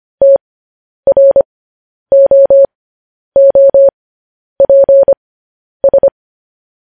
Morse code, heard as a keyed tone, about 25 wpm, spaced as for 10 wpm, 550 Hz. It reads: TROOPS